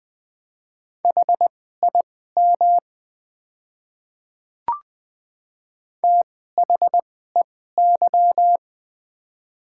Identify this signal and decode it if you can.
Morse code, heard as a keyed tone, about 20 wpm, spaced as for 14 wpm, 700 Hz. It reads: HIM THEY
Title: 2x Character Spacing - Top 500 Words - 20wpm